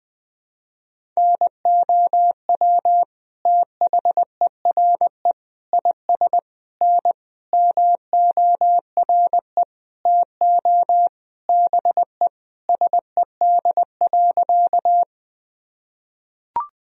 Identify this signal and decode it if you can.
Morse code, heard as a keyed tone, about 20 wpm, 700 Hz. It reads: NOW THERE IS N MORE TO BE SED.
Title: Sentences from Top 100 Words and Abbreviations 20wpm